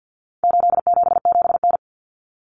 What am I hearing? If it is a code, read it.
Morse code, heard as a keyed tone, about 50 wpm, 700 Hz. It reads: 877N
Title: Common Cut Numbers - 4-Digit 50wpm